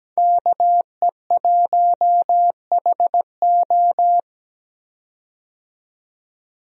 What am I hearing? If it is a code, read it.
Morse code, heard as a keyed tone, about 17 wpm, 700 Hz. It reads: KE1HO
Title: USA Call Signs - 5 Characters - 17wpm